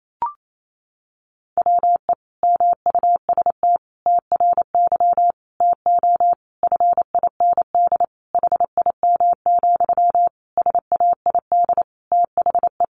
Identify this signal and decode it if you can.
Morse code, heard as a keyed tone, about 28 wpm, 700 Hz. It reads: WE MUST TRY TO FIND HIM, SAID THE
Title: Sentences from Top 400 Words 28wpm